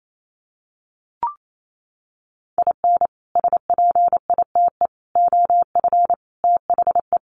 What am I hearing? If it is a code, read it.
Morse code, heard as a keyed tone, about 28 wpm, 700 Hz. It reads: IN SPITE OF THE
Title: Sets of 5 Words Encore2 28wpm